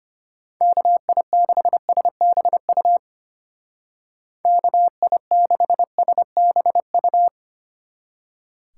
Morse code, heard as a keyed tone, about 31 wpm, 700 Hz. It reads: KI6SBU KI6SBU
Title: Speed Racing - USA 2x3 Call Signs 25wpm